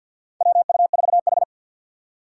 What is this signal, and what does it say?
Morse code, heard as a keyed tone, about 48 wpm, 700 Hz. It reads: WU4H
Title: USA Call Signs - 3x Character Spacing - 50wpm